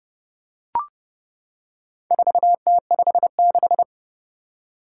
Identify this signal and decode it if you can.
Morse code, heard as a keyed tone, about 31 wpm, 700 Hz. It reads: 4T56
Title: Four Challenging Letter-Number Combinations 30wpm